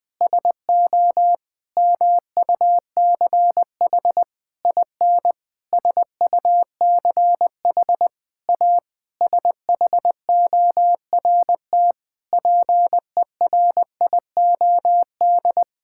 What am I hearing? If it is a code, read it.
Morse code, heard as a keyed tone, about 20 wpm, 700 Hz. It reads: SO MUCH IN SUCH A SHORT PERIOD
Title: Sentences with 7-Word Phrases - 20wpm